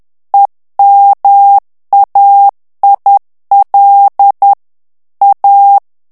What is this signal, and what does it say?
Morse code, heard as a keyed tone, about 10 wpm, 800 Hz. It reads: EMAIL A
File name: morse mp3 / ITALIEN texte clair 1 vitesse 10 mots minute